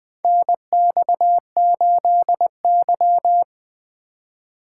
Morse code, heard as a keyed tone, about 20 wpm, 700 Hz. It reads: NX8Y
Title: USA Call Signs - 4 Characters - 4x Character Spacing - 20wpm